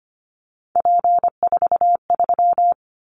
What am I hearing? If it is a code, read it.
Morse code, heard as a keyed tone, about 25 wpm, 700 Hz. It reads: P43
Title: Callsign Prefixes for Top 100 DXCC - 25wpm